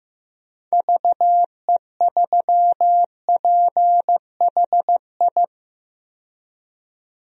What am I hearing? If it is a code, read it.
Morse code, heard as a keyed tone, about 15 wpm, 700 Hz. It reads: VE3PHI
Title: DXCC-100 Call Signs - 6 Characters - 15wpm